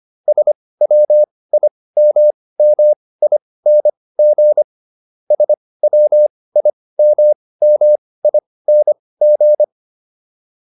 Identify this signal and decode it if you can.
Morse code, heard as a keyed tone, about 25 wpm, spaced as for 18 wpm, 600 Hz. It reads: SWIMMING SWIMMING
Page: intermediate-practice-files